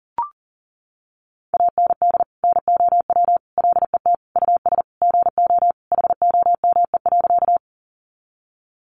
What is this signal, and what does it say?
Morse code, heard as a keyed tone, about 40 wpm, 700 Hz. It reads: AND NOW LET US GO HOME.
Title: Sentences from Top 300 Words 40wpm